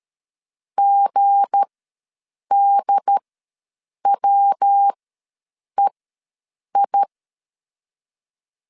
Morse code, heard as a keyed tone, about 13 wpm, spaced as for 7 wpm, 800 Hz. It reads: GDWEI
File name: LEZIONI / 7 GF9Z / 7 G ws 12wpm 3140